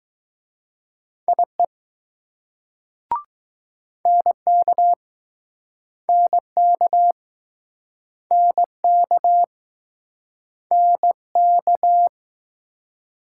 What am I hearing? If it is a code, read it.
Morse code, heard as a keyed tone, about 20 wpm, 700 Hz. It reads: IE NK NK NK NK